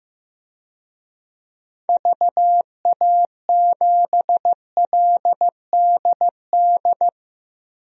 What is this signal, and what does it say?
Morse code, heard as a keyed tone, about 15 wpm, 700 Hz. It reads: VA7LDD
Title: DXCC-100 Call Signs - 6 Characters - 15wpm